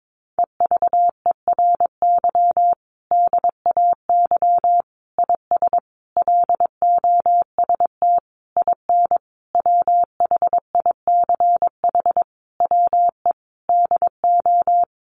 Morse code, heard as a keyed tone, about 22 wpm, 700 Hz. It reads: EVERY DAY IS LOST IN WHICH WE DO
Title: Sentences from Top 1400 Words 22wpm